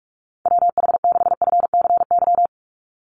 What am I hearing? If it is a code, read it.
Morse code, heard as a keyed tone, about 44 wpm, 700 Hz. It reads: WH6FCY
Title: USA Call Signs 45wpm